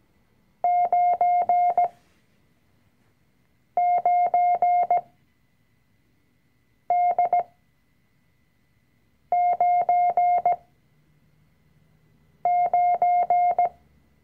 Morse code, heard as a keyed tone, about 17 wpm, spaced as for 4 wpm, 710 Hz. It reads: 99D99
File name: CW-Online-Kurs / Anfänger / Lektion 005